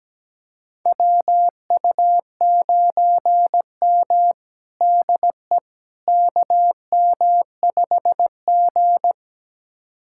Morse code, heard as a keyed tone, about 17 wpm, 700 Hz. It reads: WU9M DE KM5G
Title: USA Callsign DE Callsign - 4 Characters - 17wpm